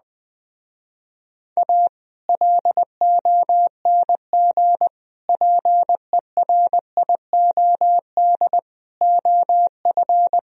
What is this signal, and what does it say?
Morse code, heard as a keyed tone, about 20 wpm, 700 Hz. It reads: A LONG PERIOD OF
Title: Speed-Racing - Sets of 5 Words 20wpm